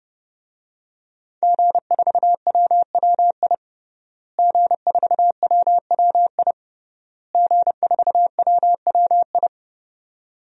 Morse code, heard as a keyed tone, about 30 wpm, 700 Hz. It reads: G4WWI G4WWI G4WWI
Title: DXCC-100 Call Signs - 5 Characters - Repeated 3x - 30wpm